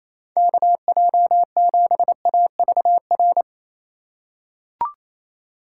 Morse code, heard as a keyed tone, about 28 wpm, 700 Hz. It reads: KJ7AVR
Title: USA Call Signs - 6 Characters - 4x Character Spacing - 28wpm